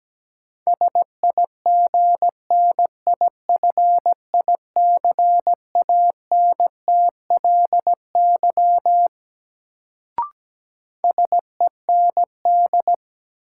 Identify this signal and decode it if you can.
Morse code, heard as a keyed tone, about 17 wpm, 700 Hz. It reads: SIGNIFICANTLY SEND
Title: Sentences from Top 1500 Words - New Word Drill - 17wpm